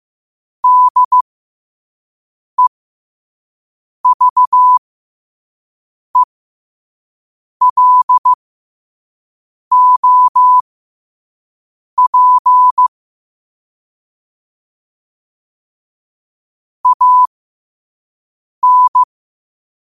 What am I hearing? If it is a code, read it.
Morse code, heard as a keyed tone, about 15 wpm, spaced as for 5 wpm, 1000 Hz. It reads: DEVELOP AN
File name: 5wpm